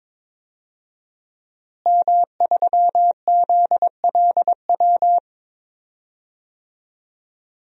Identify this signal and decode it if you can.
Morse code, heard as a keyed tone, about 22 wpm, 700 Hz. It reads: M3ZLW
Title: DXCC-100 Call Signs - 5 Characters - Repeated 2x - 22wpm